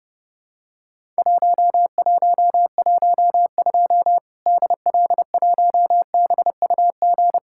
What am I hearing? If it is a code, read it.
Morse code, heard as a keyed tone, about 30 wpm, 700 Hz. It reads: 1112 DL1BUG
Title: Worked All Europe DX Contest - QTC Exchange - 30wpm